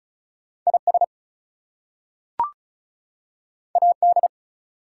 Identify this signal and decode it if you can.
Morse code, heard as a keyed tone, about 36 wpm, 700 Hz. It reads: IS AD
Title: Two Letter Words - 35wpm